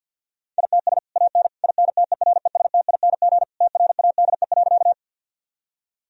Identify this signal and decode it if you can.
Morse code, heard as a keyed tone, about 50 wpm, 700 Hz. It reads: ITS AN INTERESTING TRADE.
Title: Sentences from Top 1000 Words - Review Part 4 - 50wpm